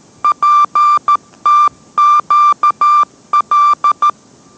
Morse code, heard as a keyed tone, about 13 wpm, 1180 Hz. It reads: PTQL